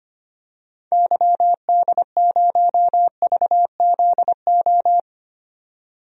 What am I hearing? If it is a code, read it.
Morse code, heard as a keyed tone, about 25 wpm, 700 Hz. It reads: YD0VZO
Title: DXCC-100 Call Signs - 6 Characters - 25wpm